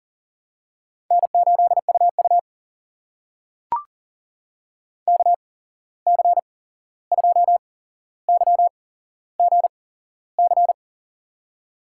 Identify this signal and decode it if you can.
Morse code, heard as a keyed tone, about 40 wpm, spaced as for 15 wpm, 700 Hz. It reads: N8UU KC2YGC
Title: USA Call Signs - 8x Character Spacing - 40wpm